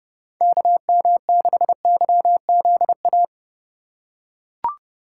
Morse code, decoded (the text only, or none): KM6YZA